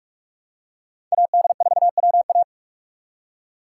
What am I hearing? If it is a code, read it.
Morse code, heard as a keyed tone, about 43 wpm, 700 Hz. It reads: AD4WA